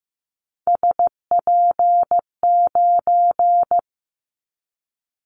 Morse code, heard as a keyed tone, about 15 wpm, 700 Hz. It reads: SP9
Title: Callsign Prefixes for Top 25 DXCC - 15wpm